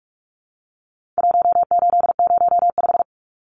Morse code, heard as a keyed tone, about 44 wpm, 700 Hz. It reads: 1805
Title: Numbers - Contest Style - 4-Digit 45wpm